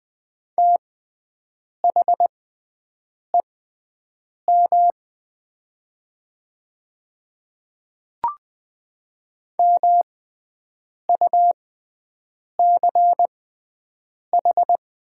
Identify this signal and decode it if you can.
Morse code, heard as a keyed tone, about 20 wpm, spaced as for 7 wpm, 700 Hz. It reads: THEM MUCH
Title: 6x Character Spacing - Top 300 Words - 20wpm